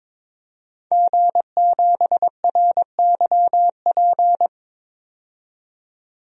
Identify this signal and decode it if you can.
Morse code, heard as a keyed tone, about 22 wpm, 700 Hz. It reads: G7RYP
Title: DXCC-100 Call Signs - 5 Characters - 22wpm